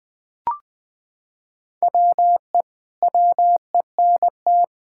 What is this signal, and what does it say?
Morse code, heard as a keyed tone, about 20 wpm, 700 Hz. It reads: WE WENT
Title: Sentences with 9-Word Phrases - 20wpm